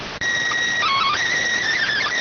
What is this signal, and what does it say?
SSTV (slow-scan television), PD 290.